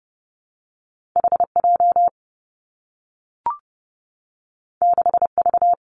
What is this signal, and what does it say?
Morse code, heard as a keyed tone, about 30 wpm, 700 Hz. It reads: H J 6 V
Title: Two Challenging Letter-Number Combinations 30wpm